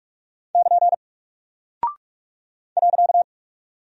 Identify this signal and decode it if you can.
Morse code, heard as a keyed tone, about 44 wpm, 700 Hz. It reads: <KN> .
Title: Single Character (Letter:Number:Punctuation:Prosign) Alt 45wpm